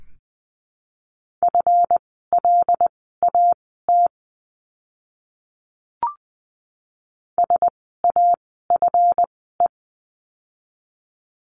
Morse code, heard as a keyed tone, about 20 wpm, spaced as for 14 wpm, 700 Hz. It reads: FLAT SAFE
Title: 139 Lesson 23f - Letter f - Words 20wpm